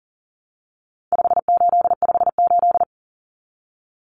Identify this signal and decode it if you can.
Morse code, heard as a keyed tone, about 40 wpm, 700 Hz. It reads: 5858
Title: Numbers - 4-Digit 40wpm